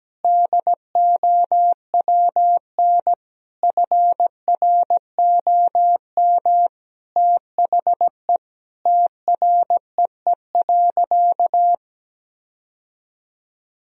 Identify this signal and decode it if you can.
Morse code, heard as a keyed tone, about 17 wpm, 700 Hz. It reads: DOWN FROM THE TREE.
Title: Sentences from Top 1000 Words - Review Part 1 - 17wpm